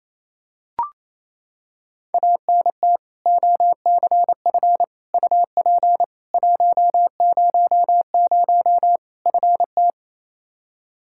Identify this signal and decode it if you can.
Morse code, heard as a keyed tone, about 28 wpm, 700 Hz. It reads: ANT OCF UP 100 FT
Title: Practice QSO - Exchange 2 - Info - 28wpm